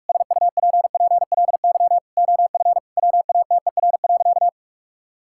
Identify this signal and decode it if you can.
Morse code, heard as a keyed tone, about 41 wpm, 700 Hz. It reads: SUPPLY OF WATER.
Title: Sentences from Top 1000 Words - Review Part 5 - 45wpm